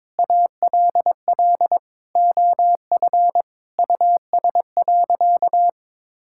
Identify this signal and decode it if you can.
Morse code, heard as a keyed tone, about 22 wpm, 700 Hz. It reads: ALL OF US.
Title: Sentences from Top 1000 Words - Review Part 6 - 22wpm